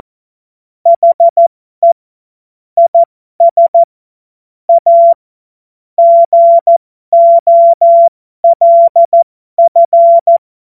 Morse code, heard as a keyed tone, about 14 wpm, spaced as for 12 wpm, 680 Hz. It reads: HE IS A GOLF
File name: Morsekurs / Witze